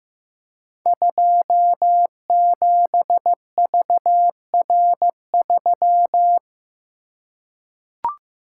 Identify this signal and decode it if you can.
Morse code, heard as a keyed tone, about 15 wpm, 700 Hz. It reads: 27VR3
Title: Code Groups - 5 Letter:Numbers - 15wpm